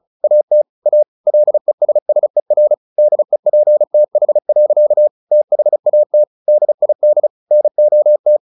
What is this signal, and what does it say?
Morse code, heard as a keyed tone, about 34 wpm, 600 Hz. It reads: AT A LESSER DEPTH. THAT DID NOT
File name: the old man and the sea / the old man and the sea 1 35wpm